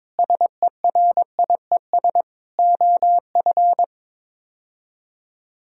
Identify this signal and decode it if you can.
Morse code, heard as a keyed tone, about 22 wpm, 700 Hz. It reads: SERIES OF
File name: practice / 22wpm / Sets of 8 Words - 22wpm